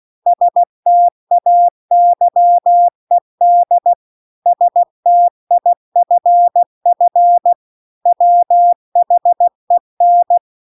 Morse code, heard as a keyed tone, about 16 wpm, 700 Hz. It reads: STAYED STIFF WHEN
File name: CW-The-Big-Twoheaded-River-2-16WPM